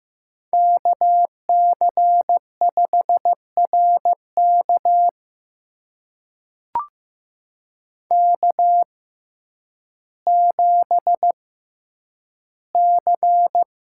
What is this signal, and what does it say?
Morse code, heard as a keyed tone, about 15 wpm, spaced as for 9 wpm, 700 Hz. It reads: KC5RK K7C
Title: USA Call Signs - 5 Characters - 6x Character Spacing - 15wpm